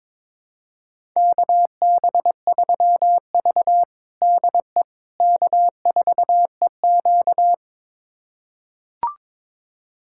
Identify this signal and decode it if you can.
Morse code, heard as a keyed tone, about 22 wpm, 700 Hz. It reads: KB3V DE K4EQ
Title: USA Callsign DE Callsign - 4 Characters - 22wpm